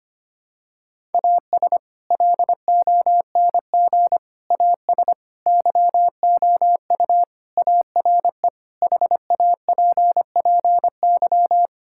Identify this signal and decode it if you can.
Morse code, heard as a keyed tone, about 25 wpm, 700 Hz. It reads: AS LONG AS YOU ARE HAPPY
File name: practice / 25wpm / Sentences with 4-Word Phrases - 25wpm